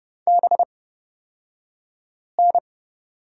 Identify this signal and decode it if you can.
Morse code, heard as a keyed tone, about 31 wpm, spaced as for 5 wpm, 700 Hz. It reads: BN
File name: Lesson 37d - Letter x - Two Characters 30wpm